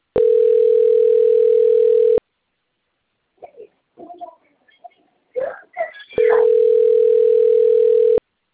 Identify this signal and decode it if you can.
Ringback tone (North America), 440 + 480 Hz, with speech.
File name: rg-1600-unknown- - - .19765